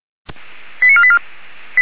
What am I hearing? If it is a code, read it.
Five-tone selective call (ZVEI), heard as quick runs of five tones, tones of about 70 ms. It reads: 89484